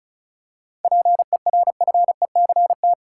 Morse code, heard as a keyed tone, about 36 wpm, 700 Hz. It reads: PERFECT
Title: Seven Letter Words - 35wpm